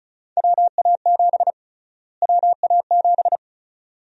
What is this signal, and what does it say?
Morse code, heard as a keyed tone, about 36 wpm, 700 Hz. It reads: WA7 WA7
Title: USA Call Sign Prefixes - Repeated 2x - 35wpm